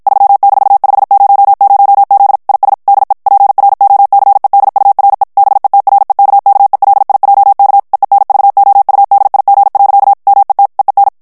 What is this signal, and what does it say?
Morse code, heard as a keyed tone, about 54 wpm, 800 Hz. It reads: 3-500Z IS DE PROCEDURE BETREKKELIJK EENVOUDIG. MET EEN
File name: HOLLANDAIS texte clair 2 vitesse 50 mots minute